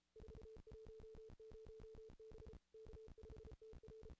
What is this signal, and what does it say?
Morse code, heard as a keyed tone, about 34 wpm, 440 Hz. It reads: 410 B K5TR
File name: contests / ss-cw-k5tr-op-n5rz -0000